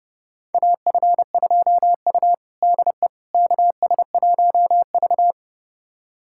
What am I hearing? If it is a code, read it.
Morse code, heard as a keyed tone, about 32 wpm, 700 Hz. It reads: AF2U DE KS1V
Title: USA Callsign DE Callsign - 4 Characters - 30wpm